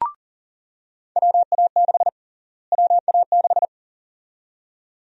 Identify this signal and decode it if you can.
Morse code, heard as a keyed tone, about 40 wpm, 700 Hz. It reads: WA6 WA6
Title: USA Call Sign Prefixes - Repeated 2x - 40wpm